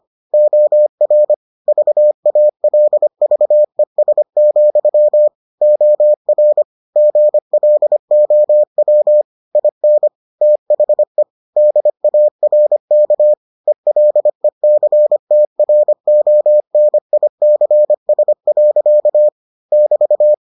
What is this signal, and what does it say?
Morse code, heard as a keyed tone, about 25 wpm, 600 Hz. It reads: OR VALVES, OR GLOW IN THE DARK ELECTRONICS. <BT>